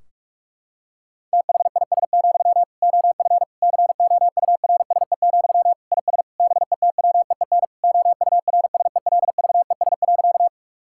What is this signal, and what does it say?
Morse code, heard as a keyed tone, about 46 wpm, 700 Hz. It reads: THIS, OF COURSE, IS BETWEEN OURSELVES.
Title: Sentences from Top 1600 Words 45wpm